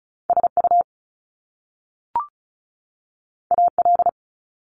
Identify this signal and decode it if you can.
Morse code, heard as a keyed tone, about 36 wpm, 700 Hz. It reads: SU AL
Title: Top Two Letter Combinations in Words 35wpm